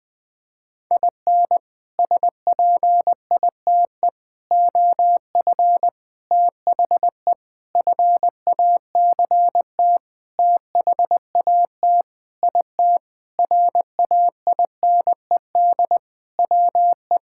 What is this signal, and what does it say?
Morse code, heard as a keyed tone, about 20 wpm, 700 Hz. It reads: IN SPITE OF THE FACT THAT IT RAINED WE
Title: Sentences with 6-Word Phrases - 20wpm